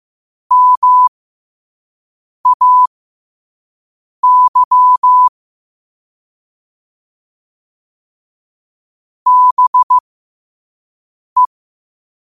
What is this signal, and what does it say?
Morse code, heard as a keyed tone, about 15 wpm, spaced as for 5 wpm, 1000 Hz. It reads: MAY BE
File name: qotdfiles / 5wpm